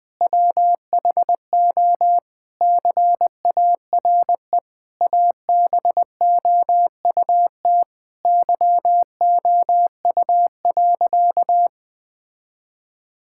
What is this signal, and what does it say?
Morse code, heard as a keyed tone, about 20 wpm, 700 Hz. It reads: WHO CARE ABOUT YOU.
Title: Sentences with 7-Word Phrases - Spoken Beginning - 20wpm